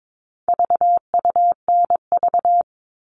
Morse code, heard as a keyed tone, about 22 wpm, 700 Hz. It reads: VUNV
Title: All Cut Numbers - 4-Digit 22wpm